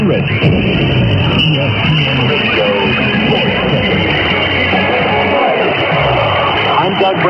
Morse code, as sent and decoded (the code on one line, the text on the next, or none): none